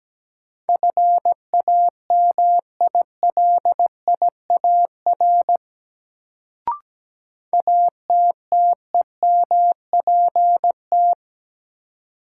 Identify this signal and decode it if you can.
Morse code, heard as a keyed tone, about 17 wpm, 700 Hz. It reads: FAMILIAR ATTEMPT
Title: Sentences from Top 1000 Words - New Word Drill - 17wpm